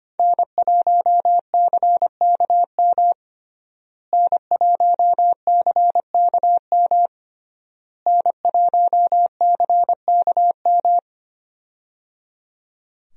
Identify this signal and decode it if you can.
Morse code, heard as a keyed tone, about 25 wpm, 700 Hz. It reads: N1CKM N1CKM N1CKM
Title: USA Call Signs - 5 Characters - Repeated 3x - 25wpm